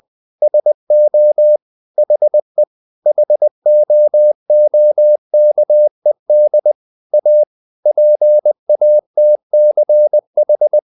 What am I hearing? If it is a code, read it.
Morse code, heard as a keyed tone, about 20 wpm, 600 Hz. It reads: SO HE HOOKED A PATCH
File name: the old man and the sea part 2 20wpm